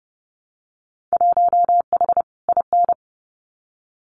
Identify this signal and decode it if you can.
Morse code, heard as a keyed tone, about 30 wpm, 700 Hz. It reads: 1H IN
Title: Winter Field Day Exchanges 30wpm